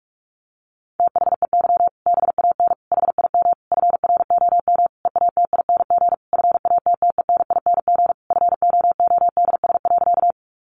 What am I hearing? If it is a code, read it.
Morse code, heard as a keyed tone, about 44 wpm, 700 Hz. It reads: THEY BAN HIM FROM EATING FATTENING FOODS.
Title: Easy Phrases 45wpm